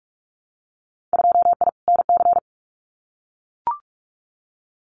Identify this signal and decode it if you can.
Morse code, heard as a keyed tone, about 44 wpm, 700 Hz. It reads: 2I NC
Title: Winter Field Day Exchanges 45wpm